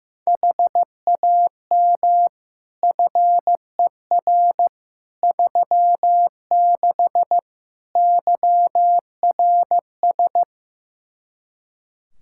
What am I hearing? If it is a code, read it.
Morse code, heard as a keyed tone, about 16 wpm, 700 Hz. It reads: HAM FER 36 YRS
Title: K1USN SST QSO Practice 15wpm